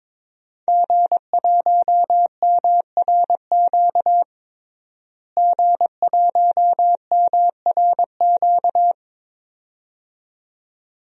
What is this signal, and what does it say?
Morse code, heard as a keyed tone, about 22 wpm, 700 Hz. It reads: G1MRQ G1MRQ
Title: DXCC-100 Call Signs - 5 Characters - Repeated 2x - 22wpm